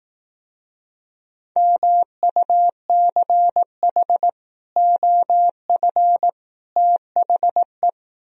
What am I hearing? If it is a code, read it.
Morse code, heard as a keyed tone, about 18 wpm, 700 Hz. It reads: MUCH OF THE
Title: Speed-Racing - Sets of 3 Words 15wpm